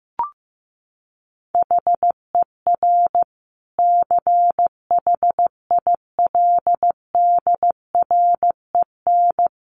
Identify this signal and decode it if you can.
Morse code, heard as a keyed tone, about 15 wpm, 700 Hz. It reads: HER CHILDREN
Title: Sentences from Top 900 Words 15wpm